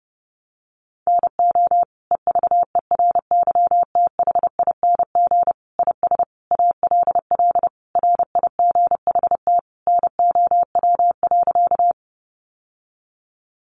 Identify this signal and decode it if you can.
Morse code, heard as a keyed tone, about 30 wpm, 700 Hz. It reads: NO EVERYTHING IS ALL RIGHT NOW.
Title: Sentences from Top 700 Words 30wpm